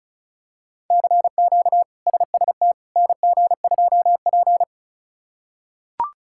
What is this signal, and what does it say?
Morse code, heard as a keyed tone, about 36 wpm, 700 Hz. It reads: CQ SST NG2P
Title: K1USN SST QSO Practice - 35wpm